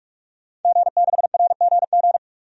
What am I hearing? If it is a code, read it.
Morse code, heard as a keyed tone, about 43 wpm, 700 Hz. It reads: G6RGG
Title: DXCC-100 Call Signs - 5 Characters - 45wpm